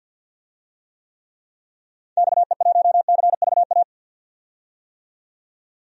Morse code, heard as a keyed tone, about 52 wpm, 700 Hz. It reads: XE1CVA
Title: DXCC-100 Call Signs - 6 Characters - 50wpm